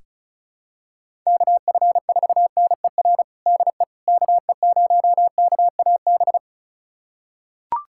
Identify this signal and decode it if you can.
Morse code, heard as a keyed tone, about 34 wpm, 700 Hz. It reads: KF4NER DE KE0KAB
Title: USA Callsign DE Callsign 35wpm